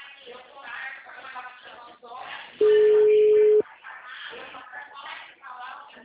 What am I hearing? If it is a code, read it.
Ringback tone (Europe, 425 Hz), with speech.